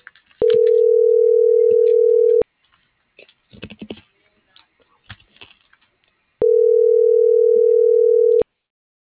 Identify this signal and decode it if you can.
Ringback tone (North America), 440 + 480 Hz.